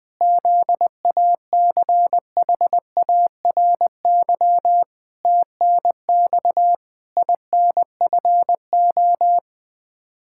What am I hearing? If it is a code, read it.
Morse code, heard as a keyed tone, about 20 wpm, 700 Hz. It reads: ZACHARY TNX INFO
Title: Practice QSO - Exchange 3 - More Info - 20wpm